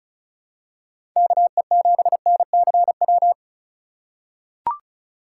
Morse code, heard as a keyed tone, about 34 wpm, 700 Hz. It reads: KE7NCW
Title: USA Call Signs - 6 Characters - 4x Character Spacing - 35wpm